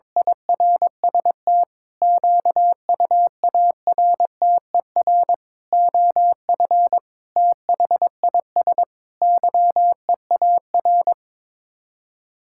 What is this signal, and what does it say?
Morse code, heard as a keyed tone, about 22 wpm, 700 Hz. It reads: IRST QUARTER OF THIS YEAR
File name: practice / 22wpm / Sets of 7 Words - 22wpm